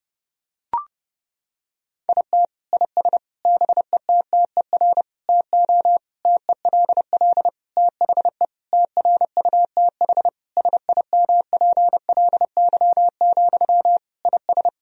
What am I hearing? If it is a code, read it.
Morse code, heard as a keyed tone, about 30 wpm, 700 Hz. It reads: IT IS BETTER TO TELL THE TRUTH SIMPLY, IS
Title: Sentences from Top 1000 Words - Review Part 4 - 30wpm